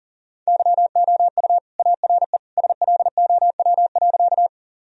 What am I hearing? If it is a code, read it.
Morse code, heard as a keyed tone, about 37 wpm, 700 Hz. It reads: YOU ARE SLOW.
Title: Sentences from Top 1000 Words - Review Part 2 - 40wpm